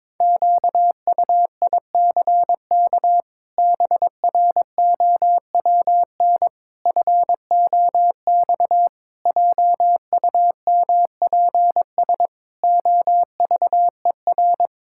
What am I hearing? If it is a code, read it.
Morse code, heard as a keyed tone, about 22 wpm, 700 Hz. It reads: QUICK BROWN FOX JUMPS OVER
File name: practice / 22wpm / Sets of 8 Words - 22wpm